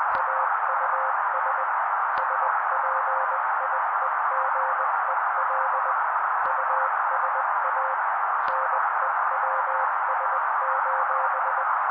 Morse code, heard as a keyed tone, about 20 wpm, 530 Hz. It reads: AUS SPIEGEL USA NEWS:
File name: daily / 18wpm noise